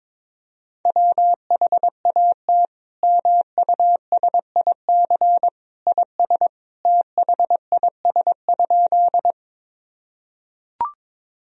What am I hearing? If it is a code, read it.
Morse code, heard as a keyed tone, about 22 wpm, 700 Hz. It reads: WHAT MUSIC IS THIS?
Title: Sentences from Top 1000 Words - Review Part 1 - 22wpm